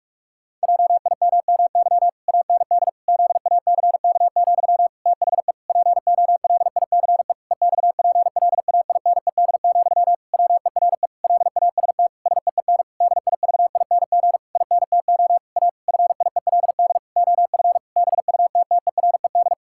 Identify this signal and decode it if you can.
Morse code, heard as a keyed tone, about 46 wpm, 700 Hz. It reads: JIMMY AND ZACK, THE POLICE EXPLAINED, WERE LAST SEEN DIVING INTO A FIELD OF BUTTERED
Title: Pangram Challenge v2 - 45wpm